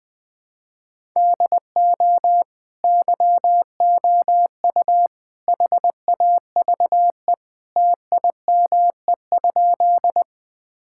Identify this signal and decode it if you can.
Morse code, heard as a keyed tone, about 20 wpm, 700 Hz. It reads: DO YOU HAVE TIME?
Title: Sentences with 3-Word Phrases - Spoken Beginning - 20wpm